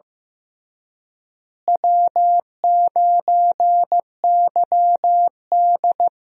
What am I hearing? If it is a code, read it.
Morse code, heard as a keyed tone, about 15 wpm, 700 Hz. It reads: W9YD
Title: USA Call Signs - 4 Characters - Repeated 3x - 15wpm